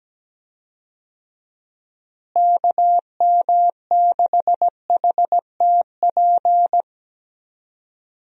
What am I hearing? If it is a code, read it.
Morse code, heard as a keyed tone, about 17 wpm, 700 Hz. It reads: KM6HTP
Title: USA Call Signs - 6 Characters - Repeated 2x - 17wpm